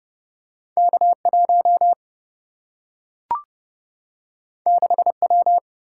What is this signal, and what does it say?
Morse code, heard as a keyed tone, about 31 wpm, 700 Hz. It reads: K1 6W
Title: Single-Letter Single-Number 30wpm